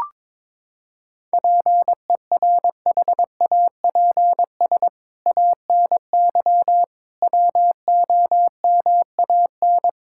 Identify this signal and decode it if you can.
Morse code, heard as a keyed tone, about 22 wpm, 700 Hz. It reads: PERHAPS ANY WOMAN
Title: Sentences from Top 1600 Words 22wpm